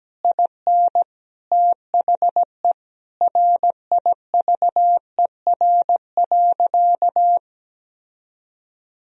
Morse code, heard as a keyed tone, about 17 wpm, 700 Hz. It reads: IN THE RIVER.
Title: Sentences from Top 1000 Words - Review Part 4 - 17wpm